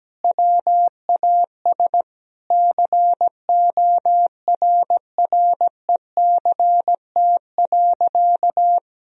Morse code, heard as a keyed tone, about 17 wpm, 700 Hz. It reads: WAS CORRECT.
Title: Sentences from Top 1000 Words - Review Part 2 - 17wpm